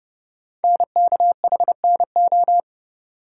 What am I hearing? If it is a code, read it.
Morse code, heard as a keyed tone, about 30 wpm, 700 Hz. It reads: NKHNO
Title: Code Groups - 5 Letters - 30wpm